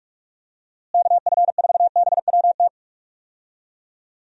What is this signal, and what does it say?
Morse code, heard as a keyed tone, about 46 wpm, 700 Hz. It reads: KF4BWT
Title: USA Call Signs - 6 Characters - Repeated 3x - 45wpm